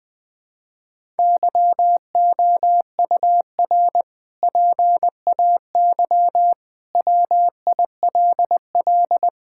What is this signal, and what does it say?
Morse code, heard as a keyed tone, about 20 wpm, 700 Hz. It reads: YOUR PAY WILL
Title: Sentences from Top 1000 Words - Review Part 6 - 20wpm